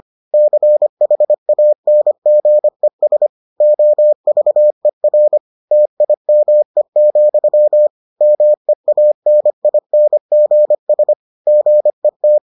Morse code, heard as a keyed tone, about 25 wpm, 600 Hz. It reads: CHANGES OVER TIME, MEANINGS GET